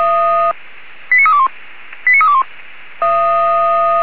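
Five-tone selective call (ZVEI), heard as quick runs of five tones, tones of about 70 ms. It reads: 89321 89321
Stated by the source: ZVEI five-tone selective call received via SDR by Feuerwehr Orlamünde.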